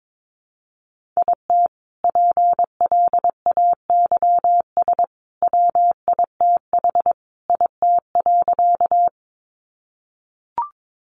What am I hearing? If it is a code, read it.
Morse code, heard as a keyed tone, about 22 wpm, 700 Hz. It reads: IT PLAYS WITH IT.
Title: Additional Easy Phrases 22wpm